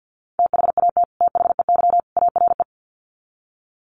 Morse code, heard as a keyed tone, about 50 wpm, 700 Hz. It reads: THAT THEY ARE
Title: Sets of 3 Words 50wpm